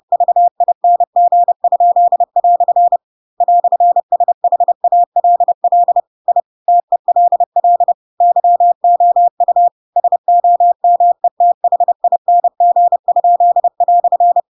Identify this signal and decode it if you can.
Morse code, heard as a keyed tone, about 30 wpm, 700 Hz. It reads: VING?" "SHALL I TELL YOU SOMETHING?"
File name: Pinocchio / Pinocchio ch4 30 30 ik1hge